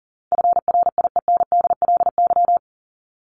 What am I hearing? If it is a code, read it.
Morse code, heard as a keyed tone, about 40 wpm, 700 Hz. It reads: FRIENDLY
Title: LY Suffix - Top 100 Words with Base Words 40wpm